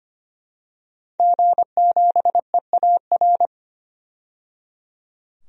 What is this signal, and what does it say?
Morse code, heard as a keyed tone, about 25 wpm, 700 Hz. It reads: G7EAR
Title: DXCC-100 Call Signs - 5 Characters - 25wpm